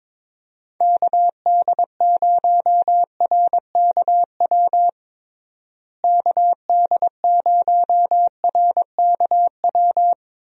Morse code, heard as a keyed tone, about 22 wpm, 700 Hz. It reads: KD0RKW KD0RKW
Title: USA Call Signs - 6 Characters - Repeated 3x - 22wpm